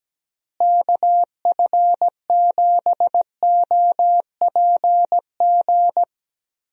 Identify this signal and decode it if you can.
Morse code, heard as a keyed tone, about 17 wpm, 700 Hz. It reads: KF7OPG
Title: USA Call Signs - 6 Characters - Repeated 3x - 17wpm